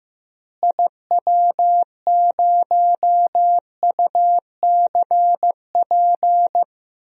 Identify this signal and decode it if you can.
Morse code, heard as a keyed tone, about 15 wpm, 700 Hz. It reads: IW0UCP
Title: DXCC-100 Call Signs - 6 Characters - Repeated 3x - 15wpm